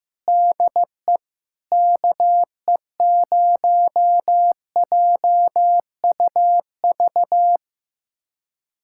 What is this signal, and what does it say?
Morse code, heard as a keyed tone, about 15 wpm, 700 Hz. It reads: DE KE0JUV
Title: USA Callsign DE Callsign 15wpm